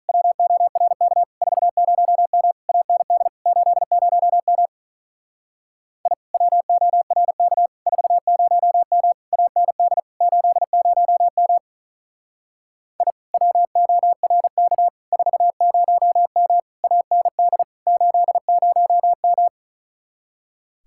Morse code, heard as a keyed tone, about 45 wpm, 700 Hz. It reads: WORK 40M AND 80M I WORK 40M AND 80M I WORK 40M AND 80M